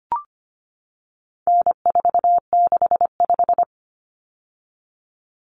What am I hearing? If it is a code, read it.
Morse code, heard as a keyed tone, about 25 wpm, 700 Hz. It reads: N465
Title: Common Cut Numbers - 4-Digit 25wpm